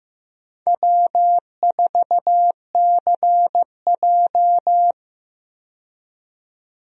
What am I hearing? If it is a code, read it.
Morse code, heard as a keyed tone, about 15 wpm, 700 Hz. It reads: W4CJ
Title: USA Call Signs - 4 Characters - 15wpm